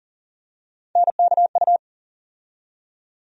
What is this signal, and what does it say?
Morse code, heard as a keyed tone, about 40 wpm, 700 Hz. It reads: NKU
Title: USA Call Sign Postfixes - 3 Letters 40wpm